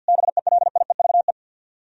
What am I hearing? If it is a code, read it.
Morse code, heard as a keyed tone, about 64 wpm, 700 Hz. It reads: BELIEVE